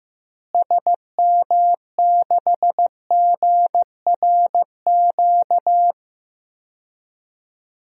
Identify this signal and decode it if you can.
Morse code, heard as a keyed tone, about 16 wpm, 700 Hz. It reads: SM6GRQ
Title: DXCC-100 Call Signs - 6 Characters - Repeated 2x - 15wpm